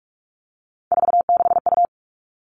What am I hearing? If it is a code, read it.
Morse code, heard as a keyed tone, about 44 wpm, 700 Hz. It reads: 46U
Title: All Cut Numbers - 3-Digit 45wpm